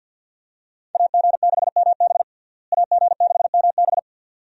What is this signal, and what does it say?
Morse code, heard as a keyed tone, about 49 wpm, 700 Hz. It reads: AG6MB AG6MB
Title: USA Call Signs - 5 Characters - Repeated 2x - 50wpm